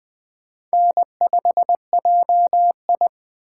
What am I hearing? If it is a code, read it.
Morse code, heard as a keyed tone, about 21 wpm, 700 Hz. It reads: N5JI